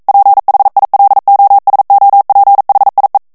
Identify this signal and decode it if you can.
Morse code, heard as a keyed tone, about 42 wpm, 800 Hz. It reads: PHILOSOPHIE
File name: CW / FICHIERS-MP3 / FRANCAIS TEXTE CLAIR 2 VITESSE 40 MOTS MINUTES